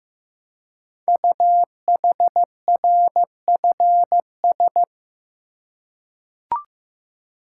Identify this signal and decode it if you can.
Morse code, heard as a keyed tone, about 15 wpm, 700 Hz. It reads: UHRFS
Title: Code Groups - 5 Letters - 15wpm